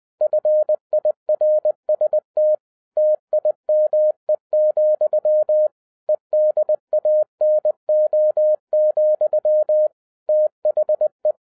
Morse code, heard as a keyed tone, about 20 wpm, 610 Hz. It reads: FIRST TIME, EDANO, THE